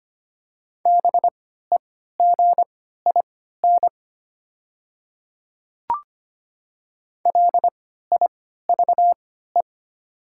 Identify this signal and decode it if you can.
Morse code, heard as a keyed tone, about 25 wpm, spaced as for 14 wpm, 700 Hz. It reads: BEGIN LIVE
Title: 3x Character Spacing - Top 300 Words - 25wpm